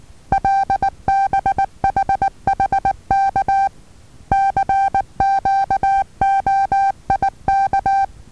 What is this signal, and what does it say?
Morse code, heard as a keyed tone, about 19 wpm, 780 Hz. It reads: LBHHK CQOIK
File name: Taviro HA6VA B